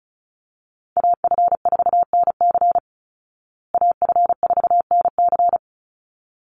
Morse code, heard as a keyed tone, about 34 wpm, 700 Hz. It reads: AF4NC AF4NC
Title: Call Signs Repeated 3x - 35wpm